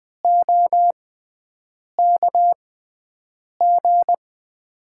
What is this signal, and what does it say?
Morse code, heard as a keyed tone, about 20 wpm, spaced as for 7 wpm, 700 Hz. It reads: OKG
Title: USA Call Signs - 5 Characters - 6x Character Spacing - 20wpm